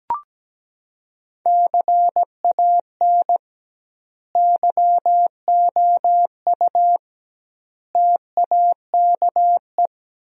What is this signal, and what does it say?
Morse code, heard as a keyed tone, about 17 wpm, 700 Hz. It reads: CAN YOU TAKE
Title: 2x Word-Spacing - Sentences from Top 200 Words - 17wpm